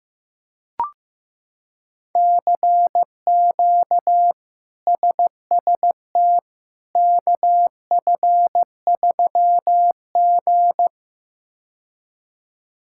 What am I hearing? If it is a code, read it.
Morse code, heard as a keyed tone, about 15 wpm, 700 Hz. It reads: CQ SST KF3G
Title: K1USN SST QSO Practice - 15wpm